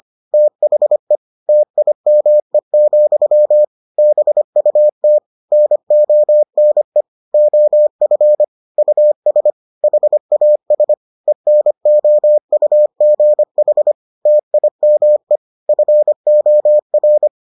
Morse code, heard as a keyed tone, about 25 wpm, 600 Hz. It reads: THE TIME, BUT NONE OF US HAS ENOUGH TIME FOR